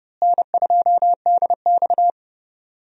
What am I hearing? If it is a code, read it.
Morse code, heard as a keyed tone, about 30 wpm, 700 Hz. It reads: N2DX
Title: USA Call Signs - 4 Characters - Repeated 3x - 30wpm